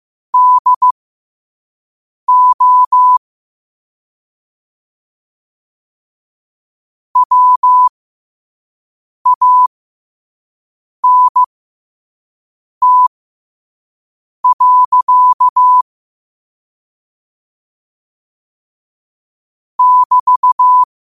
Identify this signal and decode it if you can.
Morse code, heard as a keyed tone, about 15 wpm, spaced as for 5 wpm, 1000 Hz. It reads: DO WANT. <BT>
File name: qotdfiles / 5wpm